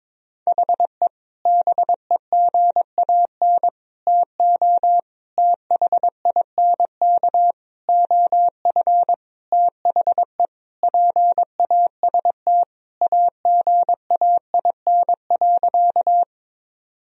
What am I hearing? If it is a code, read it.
Morse code, heard as a keyed tone, about 22 wpm, 700 Hz. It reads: HE BEGAN TO THINK OF THE PAST AGAIN.
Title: Sentences from Top 1000 Words - Review Part 3 - 22wpm